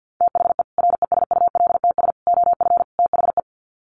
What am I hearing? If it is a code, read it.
Morse code, heard as a keyed tone, about 50 wpm, 700 Hz. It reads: THE RESULTS OF THE
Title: Sets of 4 Words 50wpm